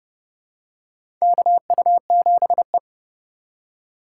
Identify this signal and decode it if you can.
Morse code, heard as a keyed tone, about 30 wpm, 700 Hz. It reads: KU7E
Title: USA Call Signs - 8x Character Spacing - 30wpm